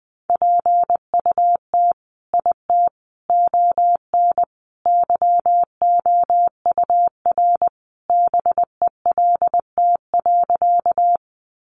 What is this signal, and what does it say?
Morse code, heard as a keyed tone, about 20 wpm, 700 Hz. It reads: PUT IT ON YOUR BELT.
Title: Sentences from Top 1400 Words 20wpm